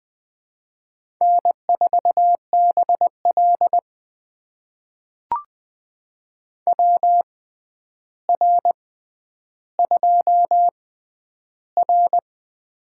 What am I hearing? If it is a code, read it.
Morse code, heard as a keyed tone, about 20 wpm, spaced as for 10 wpm, 700 Hz. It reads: N4BL WR2R
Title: USA Call Signs - 4 Characters - 6x Character Spacing - 20wpm